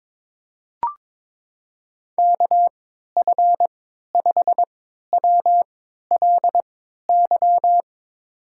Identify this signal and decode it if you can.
Morse code, heard as a keyed tone, about 21 wpm, spaced as for 12 wpm, 700 Hz. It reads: KF5WLY
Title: USA Call Signs - 6 Characters - 3x Character Spacing - 22wpm